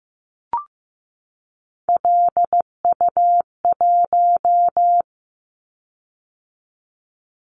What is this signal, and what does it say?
Morse code, heard as a keyed tone, about 15 wpm, 700 Hz. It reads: LU1
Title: Callsign Prefixes for Top 50 DXCC - 15wpm